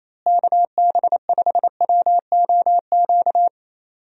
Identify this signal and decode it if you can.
Morse code, heard as a keyed tone, about 28 wpm, 700 Hz. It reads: KB5WOQ
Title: USA Call Signs - 6 Characters - Repeated 3x - 28wpm